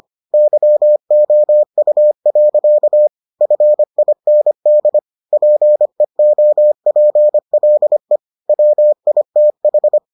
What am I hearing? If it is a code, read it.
Morse code, heard as a keyed tone, about 25 wpm, 600 Hz. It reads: YOU. FIND PEOPLE WITH